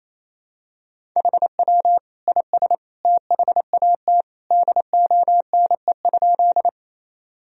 Morse code, heard as a keyed tone, about 28 wpm, 700 Hz. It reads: HW IS THAT DONE?
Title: Sentences from Top 500 Words and Abbreviations 28wpm